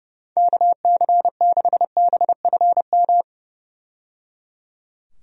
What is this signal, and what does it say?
Morse code, heard as a keyed tone, about 30 wpm, 700 Hz. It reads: KC6BFM
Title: USA Call Signs - 6 Characters - 30wpm